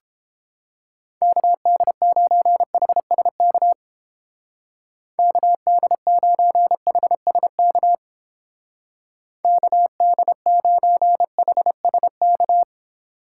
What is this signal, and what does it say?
Morse code, heard as a keyed tone, about 29 wpm, 700 Hz. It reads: KD9HSK KD9HSK KD9HSK